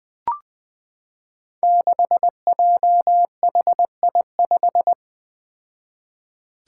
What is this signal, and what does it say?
Morse code, heard as a keyed tone, about 21 wpm, 700 Hz. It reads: 6JHI5
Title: Five Challenging Letter-Number Combinations 20wpm